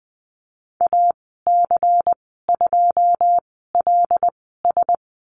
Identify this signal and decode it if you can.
Morse code, heard as a keyed tone, about 20 wpm, spaced as for 15 wpm, 700 Hz. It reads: AC2LS
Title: 154 Lesson 25g - Letter p - Call Signs 20wpm